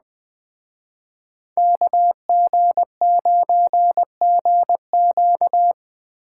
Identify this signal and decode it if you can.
Morse code, heard as a keyed tone, about 20 wpm, 700 Hz. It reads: KG9GQ
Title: Speed Racing - USA 2x2 Call Signs 17wpm